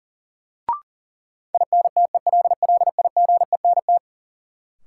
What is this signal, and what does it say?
Morse code, heard as a keyed tone, about 40 wpm, 700 Hz. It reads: INTELLIGENT
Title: Sentences from Top 1000 Words - Review Part 7 - New Word Drill - 40wpm